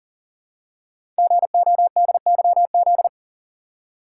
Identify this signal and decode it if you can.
Morse code, heard as a keyed tone, about 38 wpm, 700 Hz. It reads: GODYZ
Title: Code Groups - 5 Letters - 40wpm